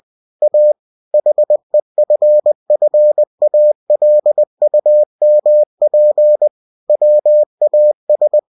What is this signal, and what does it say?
Morse code, heard as a keyed tone, about 20 wpm, 600 Hz. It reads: A HEFFALUMP WAS
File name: cw-books / winnie-the-pooh / winnie-20wpm-page-0042